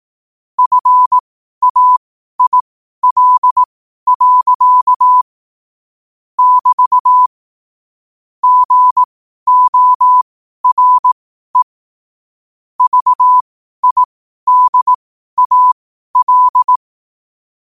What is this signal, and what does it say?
Morse code, heard as a keyed tone, about 18 wpm, spaced as for 12 wpm, 1000 Hz. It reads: FAIL. <BT> GORE VIDAL